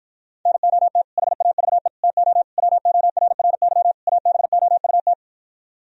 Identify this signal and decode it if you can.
Morse code, heard as a keyed tone, about 53 wpm, 700 Hz. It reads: NOT HAVE TO WORRY ABOUT